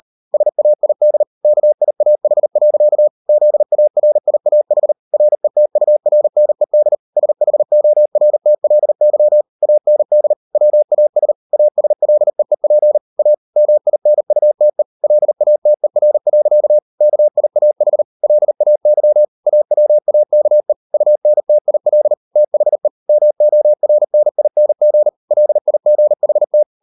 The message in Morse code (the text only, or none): SAID KIAH. ZARIAH RETURNED SHORTLY AND WAS ASLEEP A MINUTE LATER. KIAH LAY AWAKE UNTIL THE MORNING LIGHT